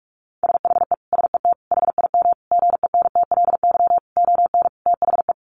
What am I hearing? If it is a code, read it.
Morse code, heard as a keyed tone, about 44 wpm, 700 Hz. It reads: SHE SET HIM GENTLY ON THE
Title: Sentences from Top 1500 Words 45wpm